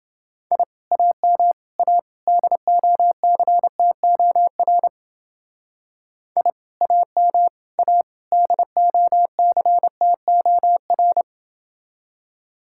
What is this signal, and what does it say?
Morse code, heard as a keyed tone, about 29 wpm, 700 Hz. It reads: I AM A DOCTOR I AM A DOCTOR